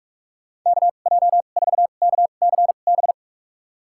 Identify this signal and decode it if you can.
Morse code, heard as a keyed tone, about 46 wpm, spaced as for 32 wpm, 700 Hz. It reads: KJ4KCB
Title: USA Call Signs - 6 Characters - 2x Character Spacing - 45wpm